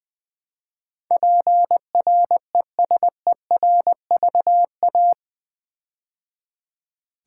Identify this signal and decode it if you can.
Morse code, heard as a keyed tone, about 20 wpm, 700 Hz. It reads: PRESERVA
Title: Build A Word - 4 Letters At A Time - 12-Letter Words - 20wpm